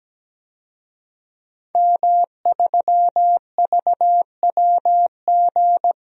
Morse code, heard as a keyed tone, about 17 wpm, 700 Hz. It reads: M3VWG
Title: DXCC-100 Call Signs - 5 Characters - 17wpm